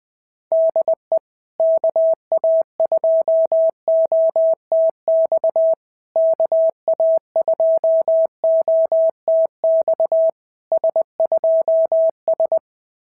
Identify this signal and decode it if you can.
Morse code, heard as a keyed tone, about 19 wpm, 650 Hz. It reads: DE KA2OTX KA2OTX S2S
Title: SOTA QSO Practice 20wpm